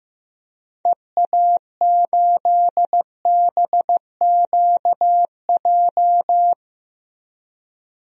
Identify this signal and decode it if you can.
Morse code, heard as a keyed tone, about 16 wpm, 700 Hz. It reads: EA8BQJ
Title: DXCC-100 Call Signs - 6 Characters - Repeated 2x - 15wpm